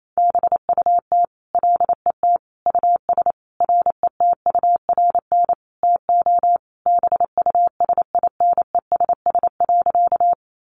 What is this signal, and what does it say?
Morse code, heard as a keyed tone, about 28 wpm, 700 Hz. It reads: BUT LET US RETURN TO BUSINESS.
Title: Sentences from Top 800 Words 28wpm